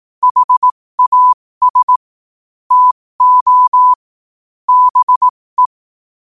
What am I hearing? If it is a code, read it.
Morse code, heard as a keyed tone, about 18 wpm, spaced as for 15 wpm, 1000 Hz. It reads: HAS TO BE